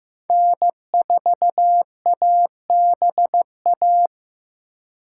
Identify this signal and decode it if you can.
Morse code, heard as a keyed tone, about 15 wpm, 700 Hz. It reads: N4ABA
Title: USA Call Signs - 5 Characters - 2x Character Spacing - 15wpm